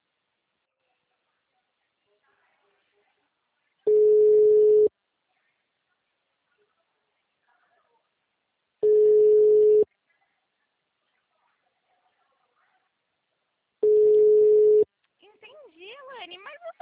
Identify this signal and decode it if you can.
Ringback tone (Europe, 425 Hz), with speech.